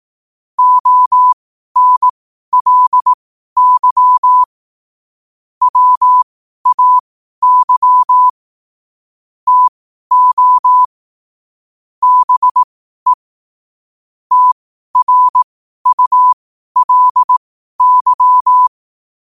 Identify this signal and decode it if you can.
Morse code, heard as a keyed tone, about 18 wpm, spaced as for 12 wpm, 1000 Hz. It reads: ONLY WAY TO BE TRULY
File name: qotdfiles / 12wpm